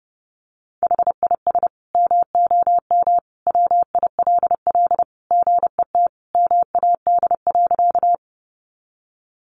Morse code, heard as a keyed tone, about 30 wpm, 700 Hz. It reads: HIS MOM WILL GET MAD.
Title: Additional Easy Phrases 30wpm